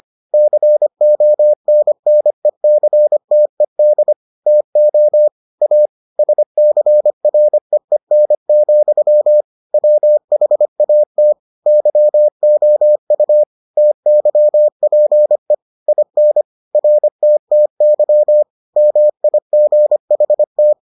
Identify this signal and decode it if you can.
Morse code, heard as a keyed tone, about 25 wpm, 600 Hz. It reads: CONNECTED TO A SCREEN, WHAT YOU TYPE IN RTTY MIGHT